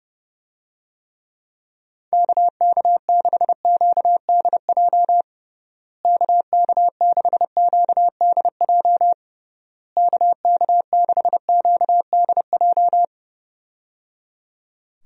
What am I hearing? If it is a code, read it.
Morse code, heard as a keyed tone, about 30 wpm, 700 Hz. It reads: KK6QDJ KK6QDJ KK6QDJ
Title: USA Call Signs - 6 Characters - Repeated 3x - 30wpm